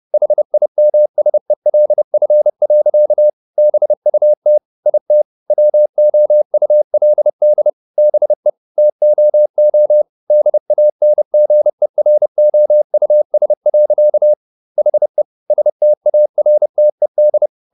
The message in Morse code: HIMSELF. BUT IT WOULD BE TOO DANGEROUS. HE STARTED

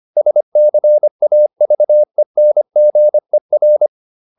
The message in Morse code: SCAVENGER

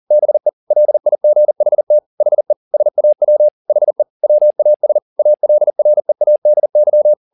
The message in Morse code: BE LIGHT HE SAW HE WAS ALREADY

